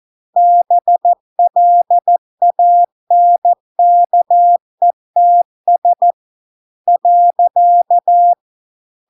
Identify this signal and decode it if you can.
Morse code, heard as a keyed tone, about 14 wpm, 700 Hz. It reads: BLANKETS .